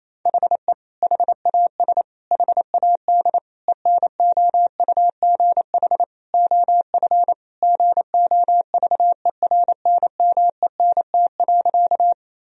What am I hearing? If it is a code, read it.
Morse code, heard as a keyed tone, about 28 wpm, 700 Hz. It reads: HE HAS HAD ENOUGH OF GOVERNMENT.
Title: Sentences from Top 1000 Words - Review Part 3 - 28wpm